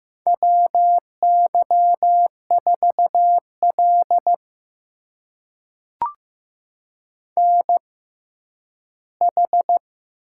Morse code, heard as a keyed tone, about 15 wpm, spaced as for 10 wpm, 700 Hz. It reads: WY4L NH
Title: USA Call Signs - 6x Character Spacing - 15wpm